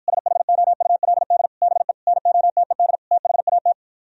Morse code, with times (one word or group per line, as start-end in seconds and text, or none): 0.1-1.5 SHOULD
1.6-1.9 BE
2.1-3.0 NOTED
3.1-3.7 THAT